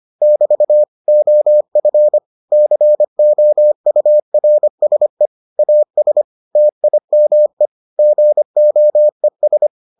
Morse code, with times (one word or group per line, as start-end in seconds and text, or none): 0.2-0.8 <BT>
1.1-2.2 OF
2.5-5.3 COURSE
5.6-6.2 AS
6.6-7.7 TIME
8.0-9.7 GOES